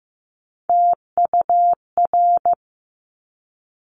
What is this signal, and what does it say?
Morse code, heard as a keyed tone, about 15 wpm, 700 Hz. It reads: TUR
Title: Top Three Letter Combinations in Words 15wpm